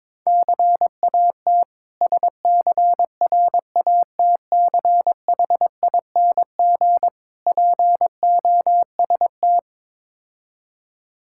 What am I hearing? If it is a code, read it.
Morse code, heard as a keyed tone, about 22 wpm, 700 Hz. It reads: CAT SCRATCHING POST
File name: Difficult Words with Hints - 22wpm